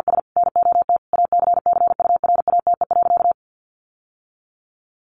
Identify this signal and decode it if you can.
Morse code, heard as a keyed tone, about 50 wpm, 700 Hz. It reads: S NOT ACCURATE.
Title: Sentences from Top 1200 Words 50wpm